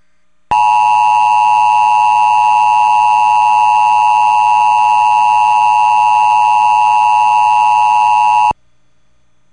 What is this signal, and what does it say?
Emergency Alert System (SAME) tones (the 853 and 960 Hz attention tone). It is heard from 0.5 to 8.6 s.